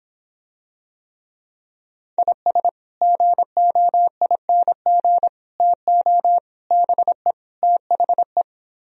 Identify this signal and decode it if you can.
Morse code, heard as a keyed tone, about 26 wpm, 700 Hz. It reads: IS GOING TO BE THE